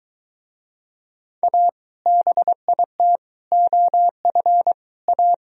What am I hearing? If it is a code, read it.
Morse code, heard as a keyed tone, about 22 wpm, 700 Hz. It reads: A BIT OF A